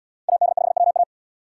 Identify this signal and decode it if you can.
Morse code, heard as a keyed tone, about 66 wpm, 700 Hz. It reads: AD5WA